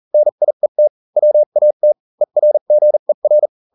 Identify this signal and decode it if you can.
Morse code, heard as a keyed tone, about 40 wpm, spaced as for 35 wpm, 600 Hz. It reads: NIET WAT ERGER